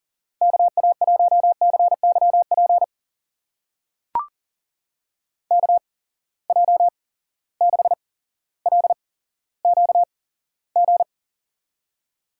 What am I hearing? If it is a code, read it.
Morse code, heard as a keyed tone, about 42 wpm, spaced as for 17 wpm, 700 Hz. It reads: KA1CYP KJ6LQG
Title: USA Call Signs - 6 Characters - 8x Character Spacing - 40wpm